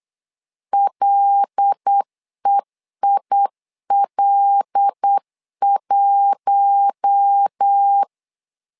Morse code, heard as a keyed tone, about 8 wpm, 800 Hz. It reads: LEIL1
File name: LEZIONI / 5 LBVC / 5 L 8wpm 2831